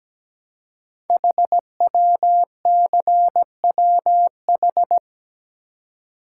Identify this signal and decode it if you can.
Morse code, heard as a keyed tone, about 17 wpm, 700 Hz. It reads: HWCWH
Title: Code Groups - 5 Letters - 17wpm